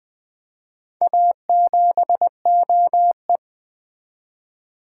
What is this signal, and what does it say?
Morse code, heard as a keyed tone, about 20 wpm, 700 Hz. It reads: A7OE
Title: Four Letter-Number Combinations - 20wpm